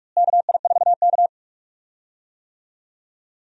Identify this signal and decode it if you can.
Morse code, heard as a keyed tone, about 46 wpm, 700 Hz. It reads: KI4K
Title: USA Call Signs - 4 Characters - Repeated 3x - 45wpm